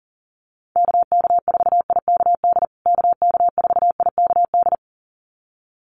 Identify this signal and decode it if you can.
Morse code, heard as a keyed tone, about 40 wpm, 700 Hz. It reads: KK4IKD KK4IKD
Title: USA Call Signs Repeated 2x 40wpm